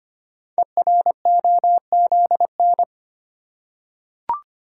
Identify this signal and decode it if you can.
Morse code, heard as a keyed tone, about 25 wpm, 700 Hz. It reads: EROZN
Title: Code Groups - 5 Letters - 25wpm